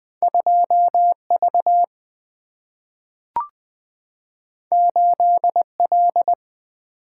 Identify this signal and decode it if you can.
Morse code, heard as a keyed tone, about 20 wpm, 700 Hz. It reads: 2 V 8 L
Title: Single-Letter Single-Number 20wpm